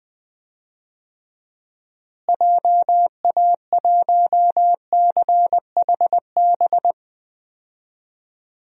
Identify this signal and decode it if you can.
Morse code, heard as a keyed tone, about 19 wpm, 700 Hz. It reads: JA1CHB
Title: DXCC-100 Call Signs - 6 Characters - Repeated 3x - 20wpm